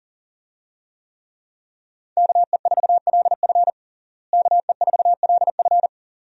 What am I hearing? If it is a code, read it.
Morse code, heard as a keyed tone, about 40 wpm, 700 Hz. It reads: KE4LF KE4LF
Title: USA Call Signs - 5 Characters - Repeated 2x - 40wpm